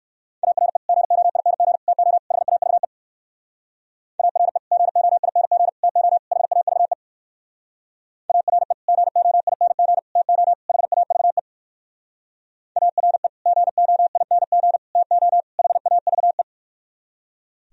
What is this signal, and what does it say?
Morse code, heard as a keyed tone, about 60 wpm, 700 Hz. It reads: ARE GOING TO HAVE ARE GOING TO HAVE ARE GOING TO HAVE ARE GOING TO HAVE